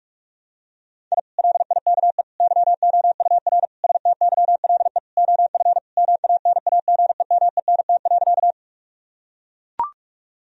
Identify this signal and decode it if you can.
Morse code, heard as a keyed tone, about 41 wpm, 700 Hz. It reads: I LIKE YOUR STYLE OF MANAGEMENT.
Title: Sentences from Top 1000 Words - Review Part 4 - 45wpm